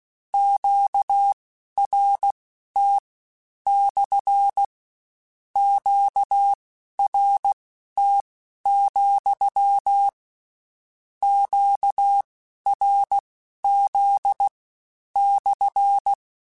Morse code, heard as a keyed tone, about 16 wpm, spaced as for 11 wpm, 800 Hz. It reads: QRT / QRT, QRZ /